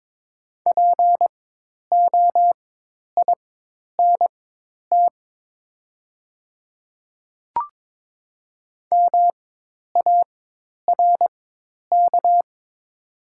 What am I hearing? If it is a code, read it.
Morse code, heard as a keyed tone, about 22 wpm, spaced as for 10 wpm, 700 Hz. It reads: POINT MARK
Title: 4x Character Spacing - Top 500 Words - 22wpm